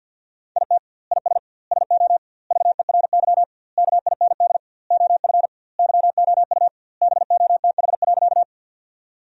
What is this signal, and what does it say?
Morse code, heard as a keyed tone, about 50 wpm, 700 Hz. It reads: IT IS SO VERY KIND OF YOU BOTH.
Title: 2x Word-Spacing - Sentences from Top 300 Words - 50wpm